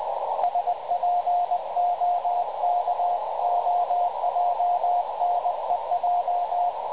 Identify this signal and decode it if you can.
Morse code, heard as a keyed tone, about 20 wpm, 710 Hz, with radio noise. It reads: SPOKOJNEJ
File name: dlazaawansowanych / nagraniazpasma / websdr recording 3567.0kHz